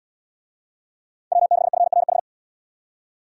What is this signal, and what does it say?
Morse code, heard as a keyed tone, about 71 wpm, 700 Hz. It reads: W6FRH